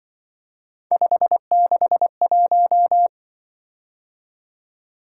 Morse code, heard as a keyed tone, about 25 wpm, 700 Hz. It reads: 561
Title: Speed-Racing - Three Letter-Number Combinations - 20wpm